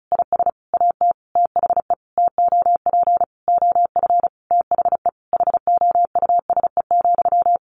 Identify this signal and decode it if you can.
Morse code, heard as a keyed tone, about 34 wpm, 700 Hz. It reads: IS AT THE TOP OF THE HOUSE,
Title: Sentences from Top 1200 Words 35wpm